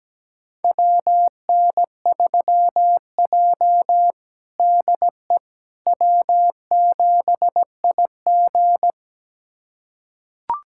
Morse code, heard as a keyed tone, about 17 wpm, 700 Hz. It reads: WN3J DE W7IG
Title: USA Callsign DE Callsign - 4 Characters - 17wpm